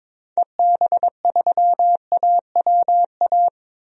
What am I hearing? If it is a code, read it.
Morse code, heard as a keyed tone, about 22 wpm, 700 Hz. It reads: EB3AWA
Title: DXCC-100 Call Signs - 6 Characters - Repeated 2x - 22wpm